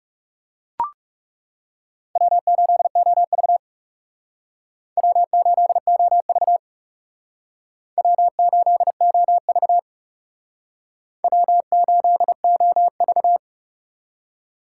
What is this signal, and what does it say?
Morse code, heard as a keyed tone, about 38 wpm, 700 Hz. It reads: W8OV W8OV W8OV W8OV
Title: Speed Racing - USA 1x2 Call Signs 30wpm